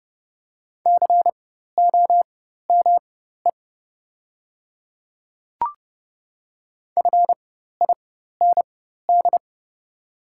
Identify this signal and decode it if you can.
Morse code, heard as a keyed tone, about 30 wpm, spaced as for 14 wpm, 700 Hz. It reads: COME FIND
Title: 4x Character Spacing - Top 200 Words - 30wpm